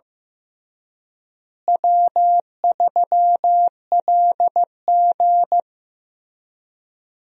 Morse code, heard as a keyed tone, about 15 wpm, 700 Hz. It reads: W3LG
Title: USA Call Signs - 4 Characters - Repeated 3x - 15wpm